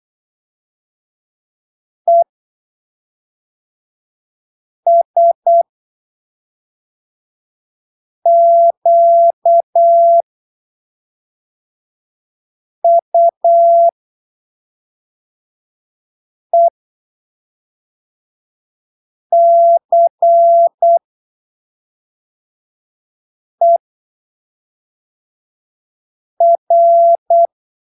Morse code, heard as a keyed tone, about 8 wpm, spaced as for 3 wpm, 680 Hz. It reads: ESQUECER
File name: arquivos / texto / texto 21 08ppm